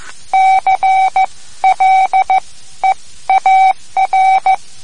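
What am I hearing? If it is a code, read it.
Morse code, heard as a keyed tone, about 15 wpm, spaced as for 12 wpm, 760 Hz. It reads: CLEAR
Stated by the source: Morse code lesson, third lesson in a series for learning the Morse alphabet.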